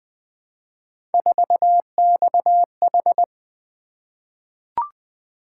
Morse code, heard as a keyed tone, about 20 wpm, 700 Hz. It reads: 4XH
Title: Three Letter-Number Combinations - 20wpm